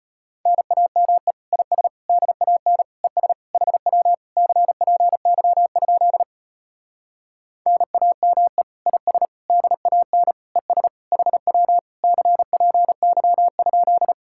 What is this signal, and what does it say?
Morse code, heard as a keyed tone, about 33 wpm, 700 Hz. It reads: NAME IS DAN ES HW CPY? NAME IS DAN ES HW CPY?